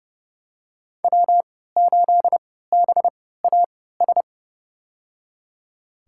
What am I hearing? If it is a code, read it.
Morse code, heard as a keyed tone, about 30 wpm, spaced as for 17 wpm, 700 Hz. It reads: W8BAS
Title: USA Call Signs - 5 Characters - 3x Character Spacing - 30wpm